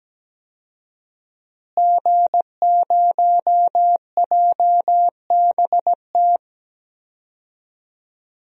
Morse code, heard as a keyed tone, about 17 wpm, 700 Hz. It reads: G0JBT
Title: DXCC-100 Call Signs - 5 Characters - Repeated 3x - 17wpm